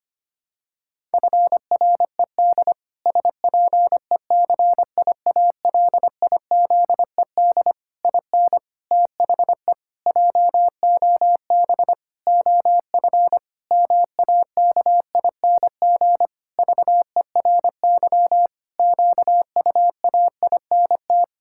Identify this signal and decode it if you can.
Morse code, heard as a keyed tone, about 25 wpm, 700 Hz. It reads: FRED SPECIALIZED IN THE JOB OF MAKING VERY QUAINT
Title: Pangram Challenge v2 - 25wpm